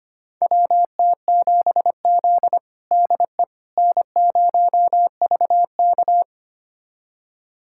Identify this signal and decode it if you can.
Morse code, heard as a keyed tone, about 25 wpm, 700 Hz. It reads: WT7Z DE N0VK
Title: USA Callsign DE Callsign - 4 Characters - 25wpm